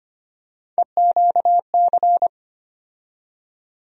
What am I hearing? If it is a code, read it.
Morse code, heard as a keyed tone, about 25 wpm, 700 Hz. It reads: EQC
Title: USA Call Sign Postfixes - 3 Letters 25wpm